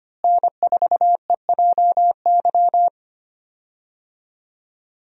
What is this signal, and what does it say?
Morse code, heard as a keyed tone, about 25 wpm, 700 Hz. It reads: N4EJY
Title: USA Call Signs - 5 Characters - Repeated 3x - 25wpm